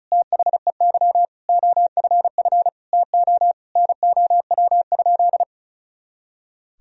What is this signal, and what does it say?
Morse code, heard as a keyed tone, about 36 wpm, 700 Hz. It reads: THEY OFF TO NOW?
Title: Sentences from Top 200 Words and Abbreviations 35wpm